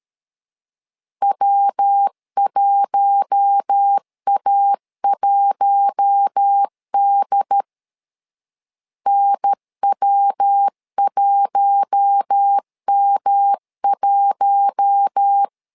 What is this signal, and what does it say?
Morse code, heard as a keyed tone, about 13 wpm, 800 Hz. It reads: W1A1D NW1M1
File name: LEZIONI / 4 Q1 12wpm 2028